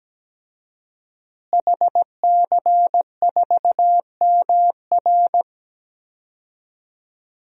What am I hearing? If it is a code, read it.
Morse code, heard as a keyed tone, about 17 wpm, 700 Hz. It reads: HC4MR
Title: DXCC-100 Call Signs - 5 Characters - Repeated 2x - 17wpm